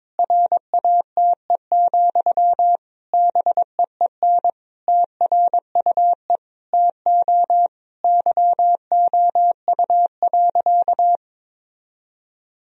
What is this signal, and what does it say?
Morse code, heard as a keyed tone, about 22 wpm, 700 Hz. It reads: RATE, BEEN TRUE TO YOU.
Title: Sentences from Top 1000 Words - Review Part 4 - 22wpm